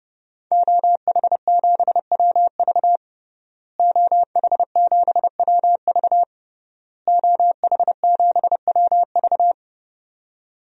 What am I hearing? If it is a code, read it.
Morse code, heard as a keyed tone, about 30 wpm, 700 Hz. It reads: OH7WV OH7WV OH7WV
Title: DXCC-100 Call Signs - 5 Characters - Repeated 3x - 30wpm